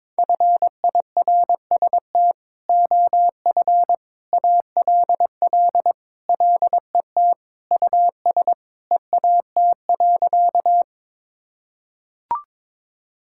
Morse code, heard as a keyed tone, about 22 wpm, 700 Hz. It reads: FIRST OF ALL LET US EAT.
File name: practice / 22wpm / Sentences with 3-Word Phrases - 22wpm